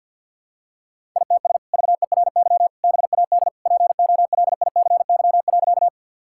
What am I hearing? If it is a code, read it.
Morse code, heard as a keyed tone, about 53 wpm, 700 Hz. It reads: ITS VERY BAD POLICY.
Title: Sentences from Top 1000 Words - Review Part 5 - 50wpm